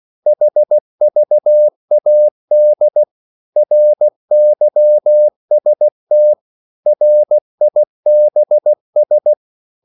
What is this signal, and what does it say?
Morse code, heard as a keyed tone, about 16 wpm, 600 Hz. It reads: HVAD RYST RIBS
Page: Korte danske ord 16wpm